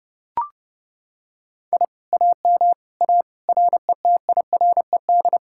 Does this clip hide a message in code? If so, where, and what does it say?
Yes: Morse code from 1.7 to 5.4 s, heard as a keyed tone, about 29 wpm, 700 Hz. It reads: I AM A RETIRED